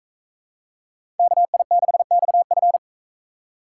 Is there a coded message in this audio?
Yes: Morse code from 1.2 to 2.8 s, heard as a keyed tone, about 41 wpm, 700 Hz. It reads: KI6XF